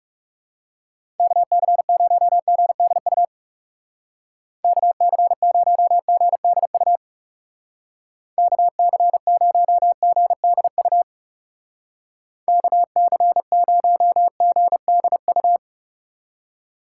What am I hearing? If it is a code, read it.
Morse code, heard as a keyed tone, about 35 wpm, 700 Hz. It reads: KC0GDU KC0GDU KC0GDU KC0GDU
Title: Speed Racing - USA 2x3 Call Signs 30wpm